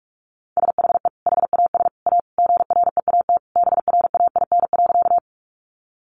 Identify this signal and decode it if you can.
Morse code, heard as a keyed tone, about 44 wpm, 700 Hz. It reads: SHE HAS A GREAT BRAIN.
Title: Sentences from Top 1100 Words 45wpm